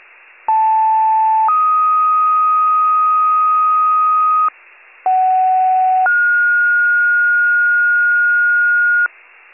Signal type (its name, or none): two-tone page